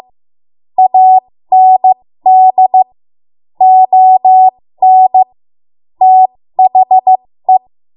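Morse code, heard as a keyed tone, about 16 wpm, spaced as for 13 wpm, 750 Hz. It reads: AND ON THE
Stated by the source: 13 wpm Morse code practice archive transmission.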